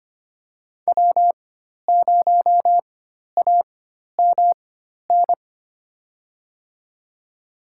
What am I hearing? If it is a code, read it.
Morse code, heard as a keyed tone, about 25 wpm, spaced as for 12 wpm, 700 Hz. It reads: W0AMN
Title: USA Call Signs - 4x Character Spacing - 25wpm